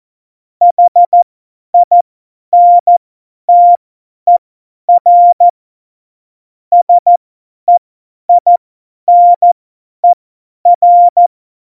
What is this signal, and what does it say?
Morse code, heard as a keyed tone, about 14 wpm, spaced as for 10 wpm, 700 Hz. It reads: HINTER SEINER
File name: Morsekurs / Witze / Witz 03.Juli 2018, 50 F70